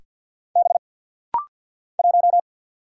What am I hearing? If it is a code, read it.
Morse code, heard as a keyed tone, about 48 wpm, 700 Hz. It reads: B 1
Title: Single Letter-Number - Rapid-Fire 50wpm